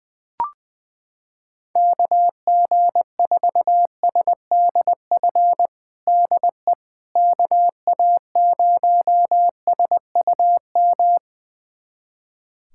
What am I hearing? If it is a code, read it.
Morse code, heard as a keyed tone, about 20 wpm, 700 Hz. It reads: KG4SDF DE KA0SUM
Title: USA Callsign DE Callsign - 6 Characters - 20wpm